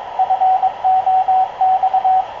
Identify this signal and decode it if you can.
Morse code, heard as a keyed tone, about 22 wpm, 750 Hz, with radio noise. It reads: FOX
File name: k0frp ki0ii wa8bxn fox utc